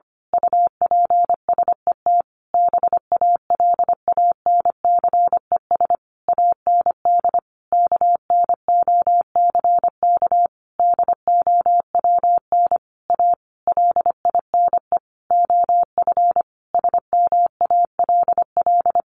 Morse code, heard as a keyed tone, about 25 wpm, 700 Hz. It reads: UPSET BALANCES AND KNOCK DOWN A LINE OF SMALL
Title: A Sound Of Thunder 25wpm